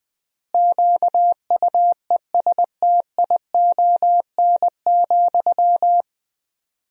Morse code, heard as a keyed tone, about 19 wpm, 700 Hz. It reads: QUESTION,
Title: 4x Word-Spacing - Sentences from Top 400 Words - 20wpm